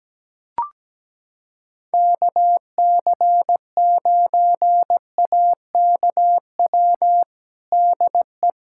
Morse code, heard as a keyed tone, about 17 wpm, 700 Hz. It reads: KC9AKW DE
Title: USA Callsign DE Callsign 17wpm